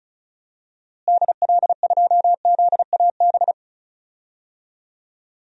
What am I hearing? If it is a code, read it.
Morse code, heard as a keyed tone, about 36 wpm, 700 Hz. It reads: DL2ZAB
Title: DXCC-100 Call Signs - 6 Characters - Repeated 3x - 35wpm